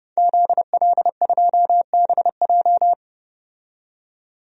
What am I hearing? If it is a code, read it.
Morse code, heard as a keyed tone, about 31 wpm, 700 Hz. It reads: ZL2BJ
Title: DXCC-100 Call Signs - 5 Characters - 30wpm